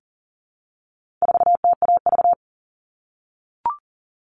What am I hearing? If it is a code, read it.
Morse code, heard as a keyed tone, about 40 wpm, 700 Hz. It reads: 4TAV
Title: All Cut Numbers - 4-Digit 40wpm